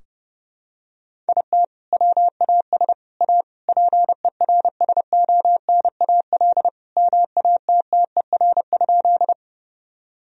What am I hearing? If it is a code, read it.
Morse code, heard as a keyed tone, about 32 wpm, 700 Hz. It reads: IT WAS A PERSONAL MATTER?
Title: Sentences from Top 1000 Words - Review Part 4 - 30wpm